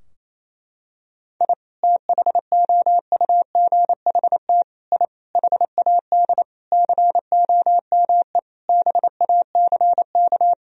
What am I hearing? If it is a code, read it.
Morse code, heard as a keyed tone, about 28 wpm, 700 Hz. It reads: I THOUGHT I HAD COME BACK
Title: Sentences from Top 1000 Words - Review Part 4 - 28wpm